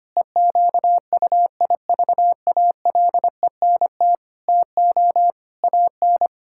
Morse code, heard as a keyed tone, about 25 wpm, 700 Hz. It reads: EQUIVALENT TO AN
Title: Sentences from Top 1600 Words 25wpm